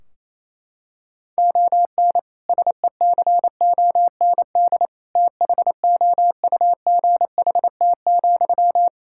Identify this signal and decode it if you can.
Morse code, heard as a keyed tone, about 28 wpm, 700 Hz. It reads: ON SECOND THOUGHT,
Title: Sentences from Top 1600 Words 28wpm